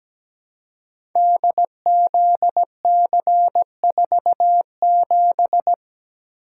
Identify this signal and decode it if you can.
Morse code, heard as a keyed tone, about 17 wpm, 700 Hz. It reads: DZC47
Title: Code Groups - 5 Letter:Numbers - 17wpm